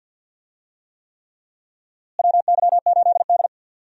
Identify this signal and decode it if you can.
Morse code, heard as a keyed tone, about 52 wpm, 700 Hz. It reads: WY8D
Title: USA Call Signs - 4 Characters - Repeated 3x - 50wpm